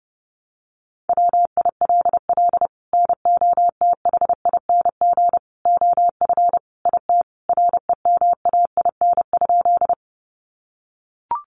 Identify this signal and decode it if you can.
Morse code, heard as a keyed tone, about 30 wpm, 700 Hz. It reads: WILL NOTHING OF IT REMAIN?
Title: Sentences from Top 1200 Words 30wpm